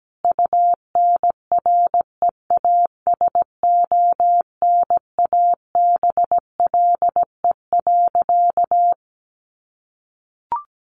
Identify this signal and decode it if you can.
Morse code, heard as a keyed tone, about 17 wpm, 700 Hz. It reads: UNREASONABLE.
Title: A Sound Of Thunder 17wpm